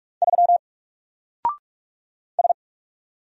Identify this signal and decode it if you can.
Morse code, heard as a keyed tone, about 46 wpm, 700 Hz. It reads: 3 S
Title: Single Letter-Number Alt 45wpm